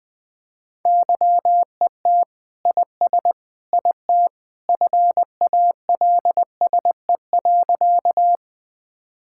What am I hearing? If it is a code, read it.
Morse code, heard as a keyed tone, about 19 wpm, 700 Hz. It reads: YET IS IT FALSE.
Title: Sentences from Top 1600 Words 20wpm